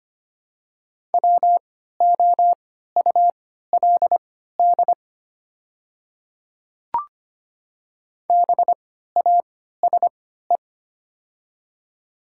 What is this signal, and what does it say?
Morse code, heard as a keyed tone, about 25 wpm, spaced as for 14 wpm, 700 Hz. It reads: WOULD BASE
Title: 3x Character Spacing - Top 300 Words - 25wpm